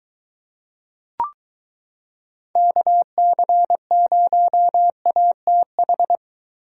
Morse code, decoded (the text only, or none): KC0ATH